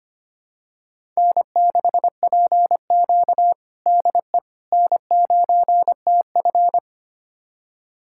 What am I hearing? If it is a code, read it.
Morse code, heard as a keyed tone, about 25 wpm, 700 Hz. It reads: N6PQ DE N9TF
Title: USA Callsign DE Callsign - 4 Characters - 25wpm